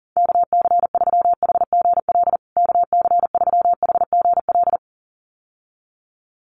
Morse code, heard as a keyed tone, about 37 wpm, 700 Hz. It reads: KC3HGL KC3HGL
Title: USA Call Signs Repeated 2x 40wpm